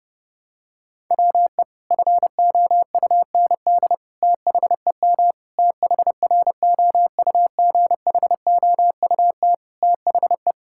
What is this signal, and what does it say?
Morse code, heard as a keyed tone, about 30 wpm, 700 Hz. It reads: WE FOUND THEM THROUGHOUT THE
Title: Sentences from Top 1000 Words - Review Part 5 - 30wpm